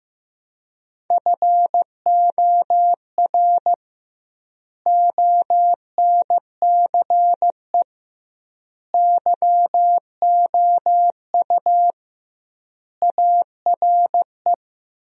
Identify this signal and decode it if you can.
Morse code, heard as a keyed tone, about 15 wpm, 700 Hz. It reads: FOR ONCE YOU ARE
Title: 2x Word-Spacing - Sentences from Top 300 Words - 15wpm